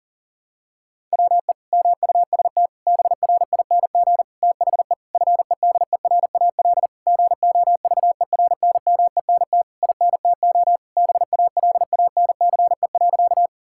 Morse code, heard as a keyed tone, about 40 wpm, 700 Hz. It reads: WE MUST BRING THE FEDERAL GOVERNMENT INTO BALANCE.
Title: Sentences from Top 1000 Words - Review Part 6 - 40wpm